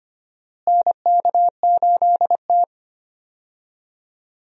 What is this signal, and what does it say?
Morse code, heard as a keyed tone, about 25 wpm, 700 Hz. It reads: NK8T
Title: USA Call Signs - 4 Characters - Repeated 3x - 25wpm